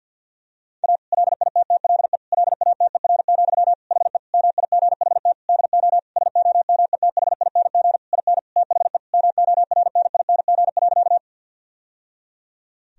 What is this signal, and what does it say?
Morse code, heard as a keyed tone, about 50 wpm, 700 Hz. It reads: A LITTLE LATER, HE MIGHT DO SOMETHING IN THE MORNING.
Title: Sentences from Top 1000 Words - Review Part 4 - 50wpm